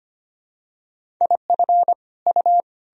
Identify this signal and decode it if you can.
Morse code, heard as a keyed tone, about 25 wpm, 700 Hz. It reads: IF U
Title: Sentences from Top 500 Words and Abbreviations 25wpm